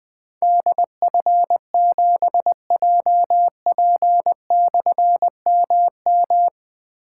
Morse code, heard as a keyed tone, about 20 wpm, 700 Hz. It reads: DF7JP/MM
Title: Difficult Call Signs 20wpm